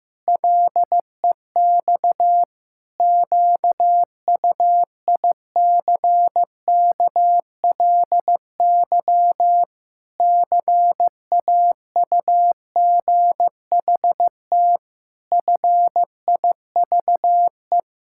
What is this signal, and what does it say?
Morse code, heard as a keyed tone, about 15 wpm, 700 Hz. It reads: LEX QUICKLY CAUGHT FIVE
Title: Pangram Challenge v2 - 15wpm